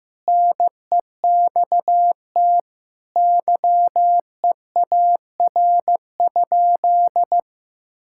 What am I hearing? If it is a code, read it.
Morse code, heard as a keyed tone, about 15 wpm, 700 Hz. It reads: NEXT YEAR?
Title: Sentences from Top 1000 Words - Review Part 5 - 15wpm